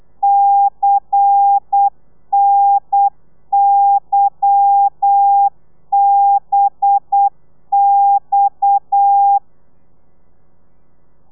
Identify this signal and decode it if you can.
Morse code, heard as a keyed tone, about 8 wpm, 800 Hz. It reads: CNYBX
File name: telegrafia / i2tfj / Lez10 35